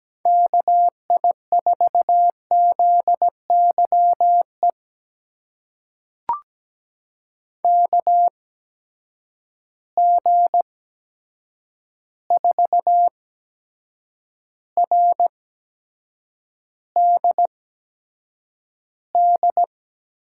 Morse code, heard as a keyed tone, about 17 wpm, spaced as for 7 wpm, 700 Hz. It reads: KI4ZYE KG4RDD
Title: USA Call Signs - 6 Characters - 8x Character Spacing - 17wpm